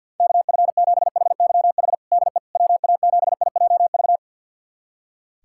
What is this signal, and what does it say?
Morse code, heard as a keyed tone, about 53 wpm, 700 Hz. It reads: KF7HYH DE WA7IJV
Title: USA Callsign DE Callsign - 6 Characters - 50wpm